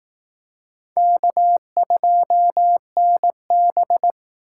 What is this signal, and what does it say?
Morse code, heard as a keyed tone, about 19 wpm, 700 Hz. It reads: K2NB
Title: Speed Racing - USA 1x2 Call Signs 15wpm